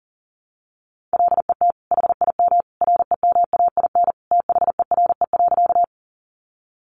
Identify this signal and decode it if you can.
Morse code, heard as a keyed tone, about 40 wpm, 700 Hz. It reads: LET HIM REMAIN THERE.
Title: Sentences from Top 1200 Words 40wpm